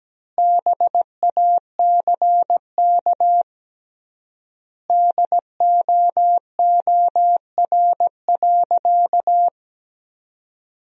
Morse code, heard as a keyed tone, about 17 wpm, 700 Hz. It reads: BACK DOOR.
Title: 3x Word-Spacing - Sentences from Top 300 Words - 17wpm